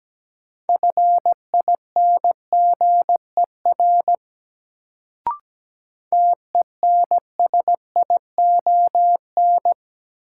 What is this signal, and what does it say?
Morse code, heard as a keyed tone, about 17 wpm, 700 Hz. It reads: FINGER TENSION
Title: Sentences from Top 1600 Words - New Word Drill Alt - 17wpm